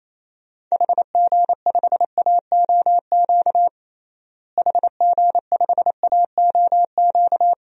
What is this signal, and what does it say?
Morse code, heard as a keyed tone, about 28 wpm, 700 Hz. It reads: HG5AOQ HG5AOQ
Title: DXCC-100 Call Signs - 6 Characters - Repeated 3x - 28wpm